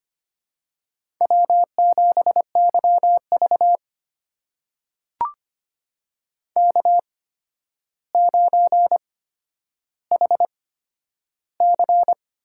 Morse code, heard as a keyed tone, about 25 wpm, spaced as for 11 wpm, 700 Hz. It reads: W7YV K9HC
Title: USA Call Signs - 8x Character Spacing - 25wpm